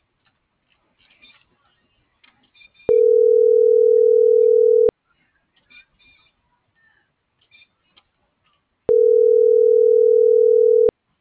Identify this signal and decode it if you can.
Ringback tone (North America), 440 + 480 Hz.